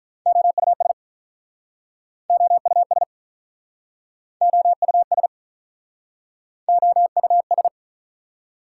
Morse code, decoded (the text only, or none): OUS OUS OUS OUS